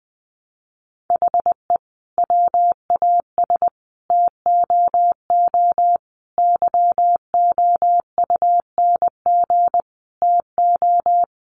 Morse code, heard as a keyed tone, about 20 wpm, 700 Hz. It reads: HE WAS TOO YOUNG TO
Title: Sentences from Top 600 Words 20wpm